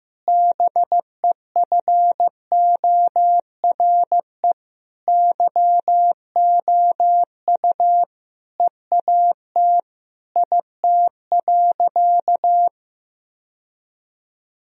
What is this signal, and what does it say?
Morse code, heard as a keyed tone, about 15 wpm, 700 Hz. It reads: BEFORE YOU EAT IT.
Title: Sentences from Top 1000 Words - Review Part 3 - 15wpm